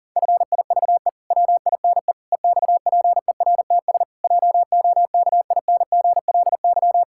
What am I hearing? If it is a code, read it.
Morse code, heard as a keyed tone, about 40 wpm, 700 Hz. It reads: FIVE WINE EXPERTS JOKINGLY
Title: Pangram Challenge v2 - 40wpm